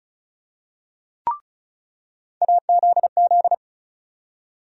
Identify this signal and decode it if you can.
Morse code, heard as a keyed tone, about 34 wpm, 700 Hz. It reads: AZZ
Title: USA Call Sign Postfixes - 3 Letters 35wpm